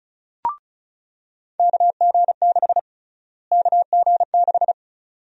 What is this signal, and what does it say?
Morse code, heard as a keyed tone, about 36 wpm, 700 Hz. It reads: KG6 KG6
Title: USA Call Sign Prefixes - Repeated 2x - 35wpm